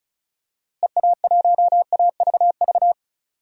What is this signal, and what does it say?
Morse code, heard as a keyed tone, about 36 wpm, 700 Hz. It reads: EA1AVV
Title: DXCC-100 Call Signs - 6 Characters - Repeated 3x - 35wpm